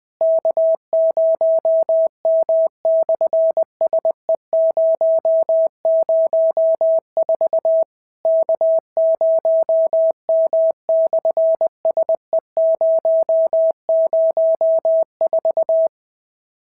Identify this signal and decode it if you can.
Morse code, heard as a keyed tone, about 21 wpm, 650 Hz. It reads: K0M/SE004 K0M/SE004
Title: SOTA QSO Practice 20wpm